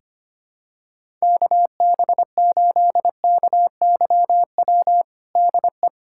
Morse code, heard as a keyed tone, about 25 wpm, 700 Hz. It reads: KB8KYW DE
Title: USA Callsign DE Callsign 25wpm